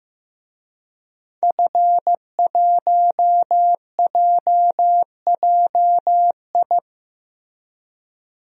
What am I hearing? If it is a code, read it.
Morse code, heard as a keyed tone, about 16 wpm, 700 Hz. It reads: F1JJI
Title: DXCC-100 Call Signs - 5 Characters - Repeated 3x - 15wpm